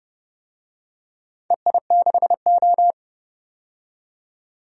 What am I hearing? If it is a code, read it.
Morse code, heard as a keyed tone, about 30 wpm, 700 Hz. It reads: EI6O
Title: DXCC-100 Call Signs - 4 Characters - 30wpm